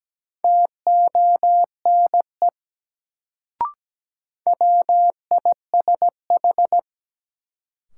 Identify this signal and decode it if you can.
Morse code, heard as a keyed tone, about 17 wpm, 700 Hz. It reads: TONE WISH
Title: Sentences from Top 1600 Words - New Word Drill Alt - 17wpm